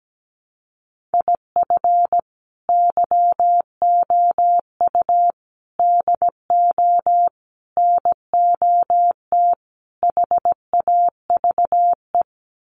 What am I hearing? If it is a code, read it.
Morse code, heard as a keyed tone, about 17 wpm, 700 Hz. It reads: IF YOU DO NOT HAVE
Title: Sets of 5 Words 17wpm